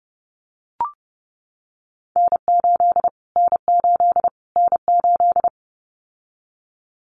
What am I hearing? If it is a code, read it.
Morse code, heard as a keyed tone, about 28 wpm, 700 Hz. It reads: N8 N8 N8
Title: Callsign Prefixes for Top 25 DXCC - Repeated 3x - 30wpm